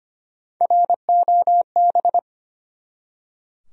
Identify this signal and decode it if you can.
Morse code, heard as a keyed tone, about 25 wpm, 700 Hz. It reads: ROB
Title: USA Common Names - 25wpm